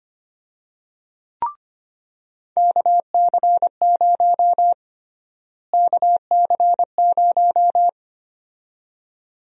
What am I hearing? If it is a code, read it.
Morse code, heard as a keyed tone, about 25 wpm, 700 Hz. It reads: KC0 KC0
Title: USA Call Sign Prefixes - Repeated 2x - 25wpm